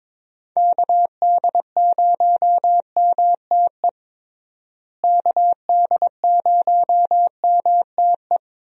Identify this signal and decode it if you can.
Morse code, heard as a keyed tone, about 22 wpm, 700 Hz. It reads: KD0MTE KD0MTE
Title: USA Call Signs - 6 Characters - Repeated 2x - 22wpm